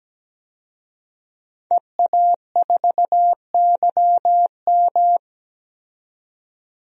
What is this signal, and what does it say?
Morse code, heard as a keyed tone, about 17 wpm, 700 Hz. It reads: EA4YM
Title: DXCC-100 Call Signs - 5 Characters - 17wpm